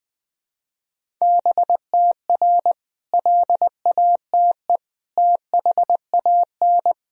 Morse code, heard as a keyed tone, about 21 wpm, 700 Hz. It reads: BTR LATE THAN
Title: Sentences from Top 400 Words and Abbreviations 20wpm